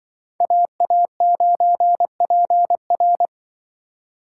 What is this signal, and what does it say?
Morse code, heard as a keyed tone, about 26 wpm, 700 Hz. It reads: AA9PR